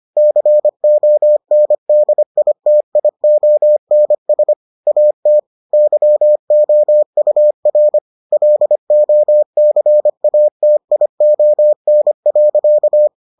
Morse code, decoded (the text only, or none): CONDITIONS AT YOUR LOCATION.